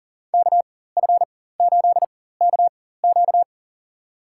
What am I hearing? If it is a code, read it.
Morse code, heard as a keyed tone, about 40 wpm, spaced as for 19 wpm, 700 Hz. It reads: KF8KQ
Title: USA Call Signs - 5 Characters - 4x Character Spacing - 40wpm